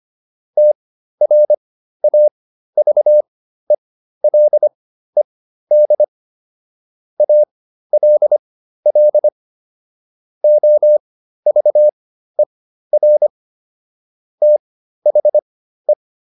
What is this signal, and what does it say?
Morse code, heard as a keyed tone, about 25 wpm, spaced as for 13 wpm, 600 Hz. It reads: TRAVELED ALL OVER THE